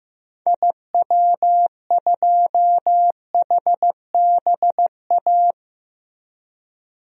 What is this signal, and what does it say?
Morse code, heard as a keyed tone, about 15 wpm, 700 Hz. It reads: IW2HBA
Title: DXCC-100 Call Signs - 6 Characters - Repeated 3x - 15wpm